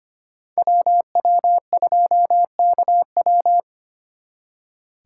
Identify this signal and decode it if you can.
Morse code, heard as a keyed tone, about 29 wpm, 700 Hz. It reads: WW2KW
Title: Speed Racing - USA 2x2 Call Signs 25wpm